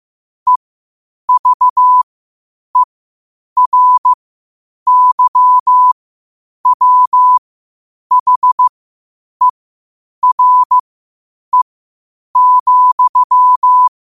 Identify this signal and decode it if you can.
Morse code, heard as a keyed tone, about 15 wpm, spaced as for 8 wpm, 1000 Hz. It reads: EVERYWHERE,
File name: qotdfiles / 8wpm